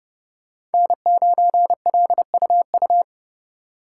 Morse code, heard as a keyed tone, about 30 wpm, 700 Hz. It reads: N9LUU
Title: USA Call Signs - 5 Characters - 2x Character Spacing - 30wpm